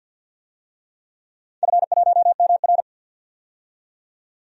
Morse code, heard as a keyed tone, about 50 wpm, 700 Hz. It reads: F1MR
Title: DXCC-100 Call Signs - 4 Characters - 50wpm